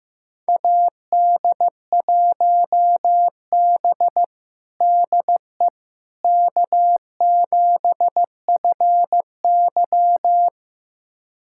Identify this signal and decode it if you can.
Morse code, heard as a keyed tone, about 15 wpm, 700 Hz. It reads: AD1B DE K7FY
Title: USA Callsign DE Callsign - 4 Characters - 15wpm